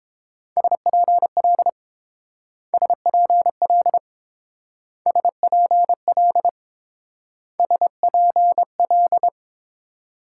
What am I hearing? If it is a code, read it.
Morse code, heard as a keyed tone, about 28 wpm, 700 Hz. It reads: SPL SPL SPL SPL